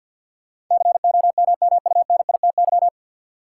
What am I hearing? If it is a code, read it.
Morse code, heard as a keyed tone, about 53 wpm, 700 Hz. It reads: COMMUNITY
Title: Sentences from Top 1200 Words - New Word Drill Alt - 50wpm